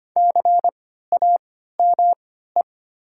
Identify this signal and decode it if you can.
Morse code, heard as a keyed tone, about 25 wpm, spaced as for 14 wpm, 700 Hz. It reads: CAME
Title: 3x Character Spacing - Top 200 Words - 25wpm